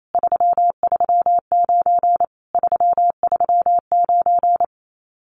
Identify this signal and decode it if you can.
Morse code, heard as a keyed tone, about 28 wpm, 700 Hz. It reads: 339 339
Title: Common QSO Phrases Encore 28wpm